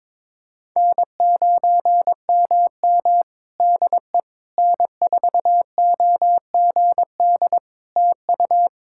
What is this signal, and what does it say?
Morse code, heard as a keyed tone, about 22 wpm, 700 Hz. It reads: N9MM DE N4OGD TU
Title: K1USN SST QSO Practice 22wpm